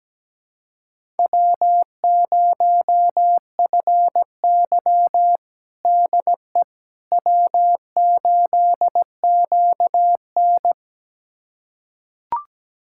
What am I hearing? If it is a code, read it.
Morse code, heard as a keyed tone, about 17 wpm, 700 Hz. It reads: W0FY DE W8QN
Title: USA Callsign DE Callsign - 4 Characters - 17wpm